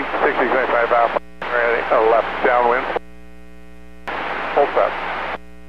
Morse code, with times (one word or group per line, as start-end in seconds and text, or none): none